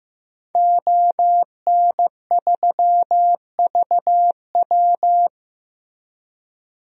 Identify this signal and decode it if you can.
Morse code, heard as a keyed tone, about 15 wpm, 700 Hz. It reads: ON3VW
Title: DXCC-100 Call Signs - 5 Characters - 15wpm